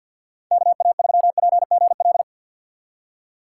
Morse code, heard as a keyed tone, about 51 wpm, 690 Hz. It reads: KA3PGL
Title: USA Call Signs - 6 Characters - 3x Character Spacing - 50wpm